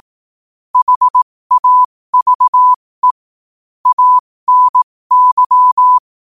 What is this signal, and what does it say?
Morse code, heard as a keyed tone, about 18 wpm, spaced as for 15 wpm, 1000 Hz. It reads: HAVE ANY
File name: qotdfiles / 15wpm